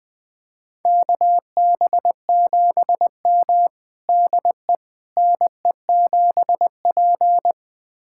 Morse code, heard as a keyed tone, about 21 wpm, 700 Hz. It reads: KB7M DE NE7P
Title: USA Callsign DE Callsign - 4 Characters - 20wpm